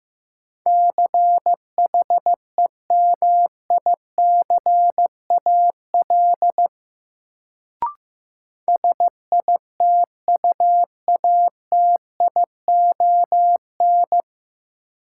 Morse code, heard as a keyed tone, about 15 wpm, 700 Hz. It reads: CHEMICAL SITUATION
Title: Sentences from Top 1200 Words - New Word Drill - 15wpm